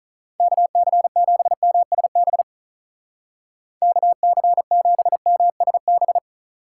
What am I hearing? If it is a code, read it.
Morse code, heard as a keyed tone, about 44 wpm, 700 Hz. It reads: KC7MSB KC7MSB
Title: Speed Racing - USA 2x3 Call Signs 35wpm